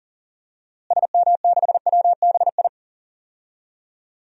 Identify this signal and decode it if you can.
Morse code, heard as a keyed tone, about 40 wpm, 700 Hz. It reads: SM6WBI
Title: DXCC-100 Call Signs - 6 Characters - Repeated 2x - 40wpm